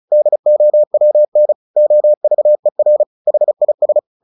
Morse code, heard as a keyed tone, about 34 wpm, 600 Hz. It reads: DOWN OVER HIS
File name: the old man and the sea / the old man and the sea 1 35wpm